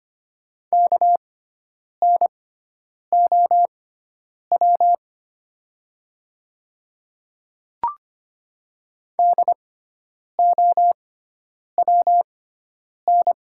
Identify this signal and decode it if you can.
Morse code, heard as a keyed tone, about 25 wpm, spaced as for 9 wpm, 700 Hz. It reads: KNOW DOWN
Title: 6x Character Spacing - Top 100 Words - 25wpm